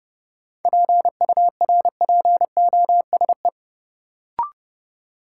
Morse code, heard as a keyed tone, about 30 wpm, 700 Hz. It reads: PURPOSE
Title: Sentences from Top 800 Words - New Word Drill - 30wpm